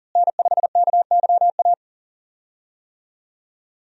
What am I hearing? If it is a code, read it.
Morse code, heard as a keyed tone, about 40 wpm, 700 Hz. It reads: N5KYA
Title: USA Call Signs - 5 Characters - Repeated 2x - 40wpm